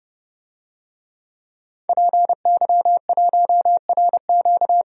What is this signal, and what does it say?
Morse code, heard as a keyed tone, about 30 wpm, 700 Hz. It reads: PY1RQ
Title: DXCC-100 Call Signs - 5 Characters - 30wpm